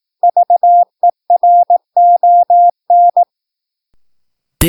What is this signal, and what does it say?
Morse code, heard as a keyed tone, about 18 wpm, 700 Hz. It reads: VERON
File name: PI4AA Uitzending